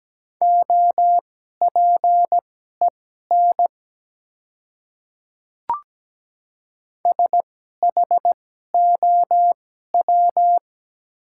Morse code, heard as a keyed tone, about 17 wpm, spaced as for 12 wpm, 700 Hz. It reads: OPEN SHOW
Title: 2x Character Spacing - Top 300 Words - 17wpm